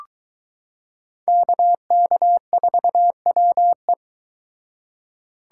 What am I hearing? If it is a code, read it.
Morse code, heard as a keyed tone, about 23 wpm, 700 Hz. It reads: KK4WE